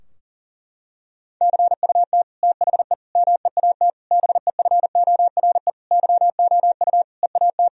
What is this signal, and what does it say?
Morse code, heard as a keyed tone, about 40 wpm, 700 Hz. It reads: CUT THE MEAT BEFORE YOU EAT
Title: Sentences from Top 1000 Words - Review Part 3 - 40wpm